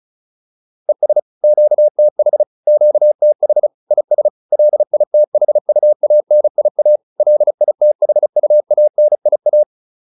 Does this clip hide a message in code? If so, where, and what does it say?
Yes: Morse code from 0.9 to 9.6 s, heard as a keyed tone, about 34 wpm, 600 Hz. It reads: ES QTH QTH IS LITHUANIA LITHUANIA